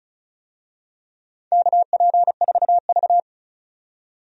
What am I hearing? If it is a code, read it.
Morse code, heard as a keyed tone, about 36 wpm, 690 Hz. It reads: KP4V
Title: DXCC-100 Call Signs - 4 Characters - 35wpm